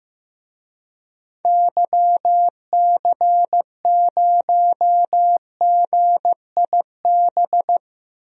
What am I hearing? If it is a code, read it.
Morse code, heard as a keyed tone, about 15 wpm, 700 Hz. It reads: YC0GIB
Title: DXCC-100 Call Signs - 6 Characters - 15wpm